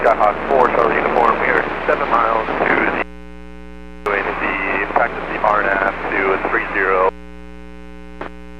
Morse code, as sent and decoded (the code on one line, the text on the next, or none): none